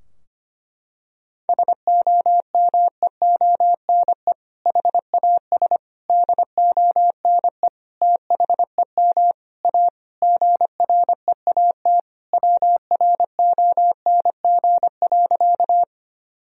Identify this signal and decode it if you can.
Morse code, heard as a keyed tone, about 25 wpm, 700 Hz. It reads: SOMEONE HAS DONE THEM A GREAT WRONG.
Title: Sentences from Top 1000 Words - Review Part 4 - 25wpm